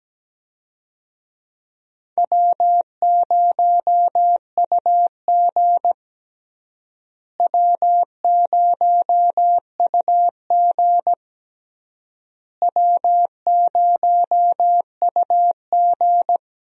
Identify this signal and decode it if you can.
Morse code, heard as a keyed tone, about 17 wpm, 700 Hz. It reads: W0UG W0UG W0UG
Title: USA Call Signs - 4 Characters - Repeated 3x - 17wpm